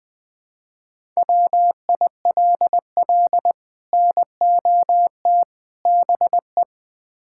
Morse code, heard as a keyed tone, about 20 wpm, 700 Hz. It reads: WILL NOT BE
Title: Build A Phrase - 5-Word Phrases - 20wpm